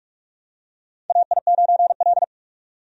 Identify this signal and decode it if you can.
Morse code, heard as a keyed tone, about 46 wpm, 700 Hz. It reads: AI9L
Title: USA Call Signs - 4 Characters - 45wpm